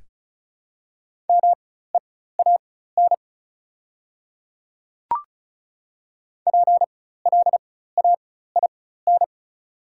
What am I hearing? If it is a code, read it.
Morse code, heard as a keyed tone, about 36 wpm, spaced as for 16 wpm, 690 Hz. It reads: MEAN PLAIN
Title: 4x Character Spacing - Top 500 Words - 35wpm